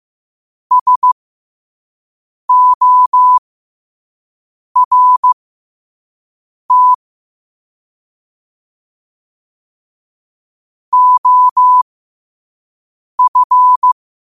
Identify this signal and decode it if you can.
Morse code, heard as a keyed tone, about 15 wpm, spaced as for 5 wpm, 1000 Hz. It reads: SORT OF